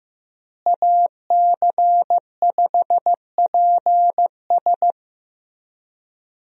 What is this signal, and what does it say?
Morse code, heard as a keyed tone, about 15 wpm, 700 Hz. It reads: AC5PS
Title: USA Call Signs - 5 Characters - Repeated 3x - 15wpm